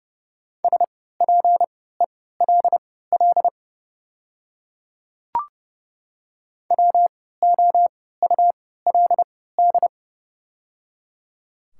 Morse code, heard as a keyed tone, about 30 wpm, spaced as for 17 wpm, 700 Hz. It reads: SPELL WOULD
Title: 3x Character Spacing - Top 200 Words - 30wpm